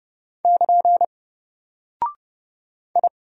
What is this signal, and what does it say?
Morse code, heard as a keyed tone, about 29 wpm, 700 Hz. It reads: <KN> I
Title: Single Character (Letter:Number:Punctuation:Prosign) 30wpm